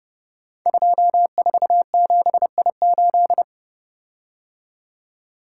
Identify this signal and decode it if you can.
Morse code, heard as a keyed tone, about 30 wpm, 700 Hz. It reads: 247I8
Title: Code Groups - 5 Letter:Numbers - 30wpm